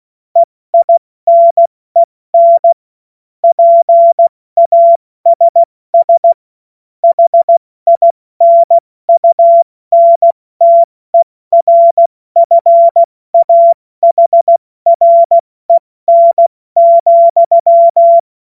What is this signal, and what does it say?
Morse code, heard as a keyed tone, about 16 wpm, 680 Hz. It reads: EINEN PASS HINUNTERFAHREN,
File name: Morsekurs / Witze / Witz 14. Dezember 2021, 70 F80